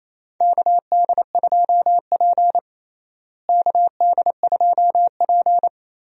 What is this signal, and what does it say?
Morse code, heard as a keyed tone, about 28 wpm, 700 Hz. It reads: KD2P KD2P
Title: USA Call Signs - 4 Characters - Repeated 3x - 28wpm